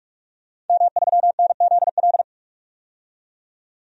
Morse code, heard as a keyed tone, about 46 wpm, 700 Hz. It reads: M3NZL
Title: DXCC-100 Call Signs - 5 Characters - 45wpm